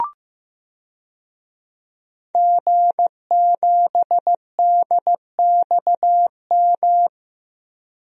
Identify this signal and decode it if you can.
Morse code, heard as a keyed tone, about 15 wpm, 700 Hz. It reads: G7DXM
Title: DXCC-100 Call Signs - 5 Characters - Repeated 2x - 15wpm